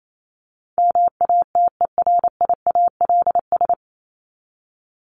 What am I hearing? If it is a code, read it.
Morse code, heard as a keyed tone, about 28 wpm, 700 Hz. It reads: MATERIALS
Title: ES:S Suffix - Top 100 Words with Base Words 28wpm